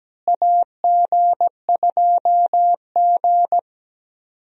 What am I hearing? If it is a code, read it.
Morse code, heard as a keyed tone, about 17 wpm, 700 Hz. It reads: AG2G
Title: USA Call Signs - 4 Characters - Repeated 3x - 17wpm